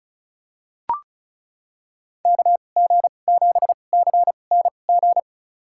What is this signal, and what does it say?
Morse code, heard as a keyed tone, about 36 wpm, spaced as for 25 wpm, 700 Hz. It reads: KG7CNG
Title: USA Call Signs - 2x Character Spacing - 35wpm